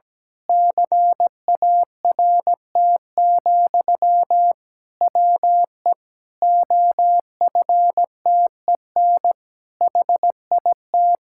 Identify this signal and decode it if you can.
Morse code, heard as a keyed tone, about 17 wpm, 700 Hz. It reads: CART, WE OFTEN HIT
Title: Pangram Challenge v2 - 17wpm